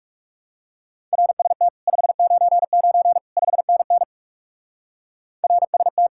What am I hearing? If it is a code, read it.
Morse code, heard as a keyed tone, about 38 wpm, 700 Hz. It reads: RST 599 5NN RST